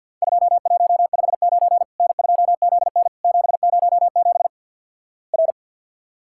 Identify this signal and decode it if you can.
Morse code, heard as a keyed tone, about 53 wpm, 700 Hz. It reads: 2159 N2ZN 707 R
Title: Worked All Europe DX Contest - QTC Exchange - 50wpm